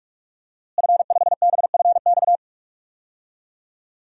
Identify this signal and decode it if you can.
Morse code, heard as a keyed tone, about 44 wpm, 700 Hz. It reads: F5BFX
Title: DXCC-100 Call Signs - 5 Characters - Repeated 3x - 45wpm